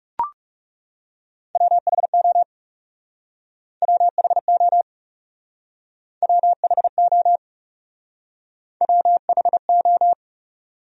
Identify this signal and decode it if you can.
Morse code, heard as a keyed tone, about 37 wpm, 700 Hz. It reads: WHO WHO WHO WHO
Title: Speed-Racing Top 100 Words 30wpm